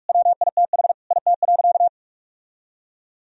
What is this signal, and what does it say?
Morse code, heard as a keyed tone, about 46 wpm, 700 Hz. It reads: WITH IT.
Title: Sentences from Top 1000 Words - Review Part 2 - 45wpm